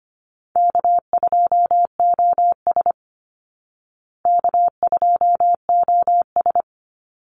Morse code, heard as a keyed tone, about 25 wpm, 700 Hz. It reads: K2OS K2OS
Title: Easy Call Signs Repeated 2x - 25wpm